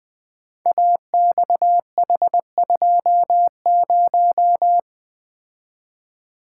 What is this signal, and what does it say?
Morse code, heard as a keyed tone, about 20 wpm, 700 Hz. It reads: AXH20
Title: Code Groups - 5 Letter:Numbers - 20wpm